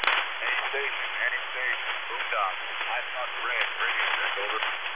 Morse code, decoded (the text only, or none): none